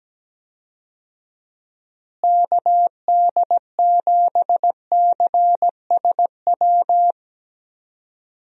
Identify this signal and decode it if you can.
Morse code, heard as a keyed tone, about 17 wpm, 700 Hz. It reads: KD7CSW
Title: USA Call Signs - 6 Characters - Repeated 3x - 17wpm